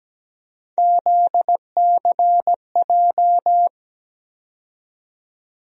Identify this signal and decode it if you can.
Morse code, heard as a keyed tone, about 17 wpm, 700 Hz. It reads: ZCJ
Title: USA Call Sign Postfixes - 3 Letters 17wpm